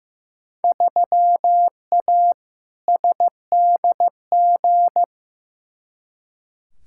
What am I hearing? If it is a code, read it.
Morse code, heard as a keyed tone, about 15 wpm, 700 Hz. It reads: 3A SDG
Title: Field Day Exchanges - 15wpm